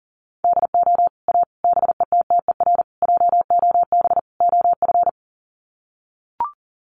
Morse code, heard as a keyed tone, about 37 wpm, 700 Hz. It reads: DO A BETTER JOB OF
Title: Sets of 5 Words Encore2 40wpm